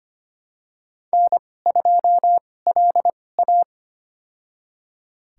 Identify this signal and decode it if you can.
Morse code, heard as a keyed tone, about 25 wpm, spaced as for 18 wpm, 700 Hz. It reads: N2LA
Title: USA Call Signs - 4 Characters - 2x Character Spacing - 25wpm